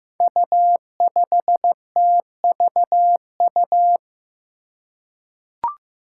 Morse code, heard as a keyed tone, about 15 wpm, 700 Hz. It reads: U5TVU
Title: Five Challenging Letter-Number Combinations 15wpm